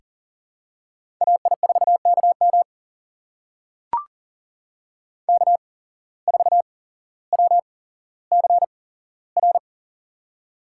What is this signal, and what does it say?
Morse code, heard as a keyed tone, about 39 wpm, spaced as for 17 wpm, 700 Hz. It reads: AI4KM K4WCR
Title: USA Call Signs - 5 Characters - 8x Character Spacing - 40wpm